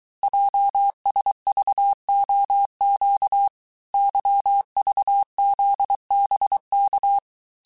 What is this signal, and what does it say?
Morse code, heard as a keyed tone, about 23 wpm, 800 Hz. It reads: JSVOQ YVZBK
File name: cours-du-mercredi / audios / cours-855-texte-chiffre-vitesse-22-mpm